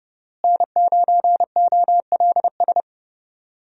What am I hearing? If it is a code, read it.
Morse code, heard as a keyed tone, about 30 wpm, 700 Hz. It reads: N9OLS
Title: USA Call Signs - 5 Characters - Repeated 2x - 30wpm